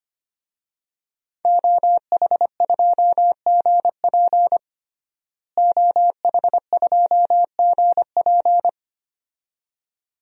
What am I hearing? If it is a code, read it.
Morse code, heard as a keyed tone, about 25 wpm, 700 Hz. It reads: OH2GP OH2GP
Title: DXCC-100 Call Signs - 5 Characters - Repeated 2x - 25wpm